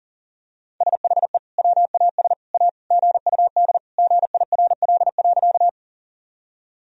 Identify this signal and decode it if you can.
Morse code, heard as a keyed tone, about 40 wpm, 700 Hz. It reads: SHE WAS A GUD GIRL.
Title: Sentences from Top 400 Words and Abbreviations 40wpm